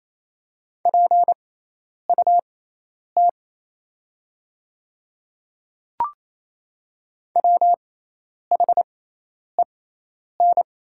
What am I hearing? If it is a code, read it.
Morse code, heard as a keyed tone, about 28 wpm, spaced as for 10 wpm, 690 Hz. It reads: PUT WHEN
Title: 6x Character Spacing - Top 200 Words - 28wpm